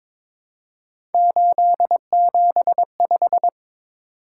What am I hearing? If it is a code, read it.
Morse code, heard as a keyed tone, about 22 wpm, 700 Hz. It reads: 875
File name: practice / 22wpm / Three Letter-Number Combinations - 22wpm